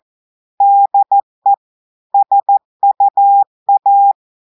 Morse code, heard as a keyed tone, about 14 wpm, 800 Hz. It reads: DE SUA
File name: audios / portugais / portugais texte en clair 1 vitesse 14 mots minutes